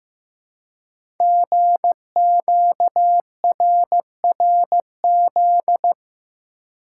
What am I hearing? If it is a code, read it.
Morse code, heard as a keyed tone, about 15 wpm, 700 Hz. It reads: GQRRZ
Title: Code Groups - 5 Letters - 15wpm